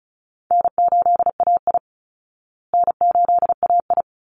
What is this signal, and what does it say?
Morse code, heard as a keyed tone, about 34 wpm, 700 Hz. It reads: N8AI N8AI
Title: Call Signs Repeated 2x - 35wpm